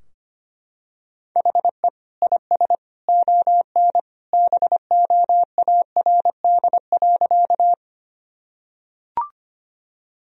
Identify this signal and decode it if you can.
Morse code, heard as a keyed tone, about 25 wpm, 700 Hz. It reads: HE IS ON BOARD.
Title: Sentences from Top 1000 Words - Review Part 4 - 25wpm